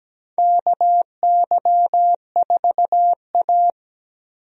Morse code, heard as a keyed tone, about 17 wpm, 700 Hz. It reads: KY4A
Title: USA Call Signs - 4 Characters - 17wpm